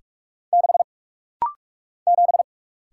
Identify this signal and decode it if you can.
Morse code, heard as a keyed tone, about 44 wpm, 700 Hz. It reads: 6 7
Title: Single Letter-Number Alt - Rapid-Fire 45wpm